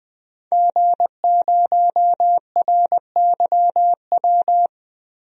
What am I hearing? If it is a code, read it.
Morse code, heard as a keyed tone, about 20 wpm, 700 Hz. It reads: G0RYW
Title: DXCC-100 Call Signs - 5 Characters - 20wpm